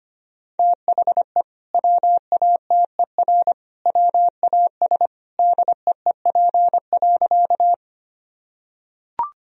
Morse code, heard as a keyed tone, about 25 wpm, 700 Hz. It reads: THE WATER WAS DEEP.
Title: Sentences from Top 1000 Words - Review Part 3 - 25wpm